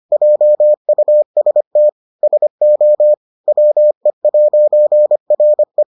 Morse code, heard as a keyed tone, about 25 wpm, 600 Hz. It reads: JUST SO WE'RE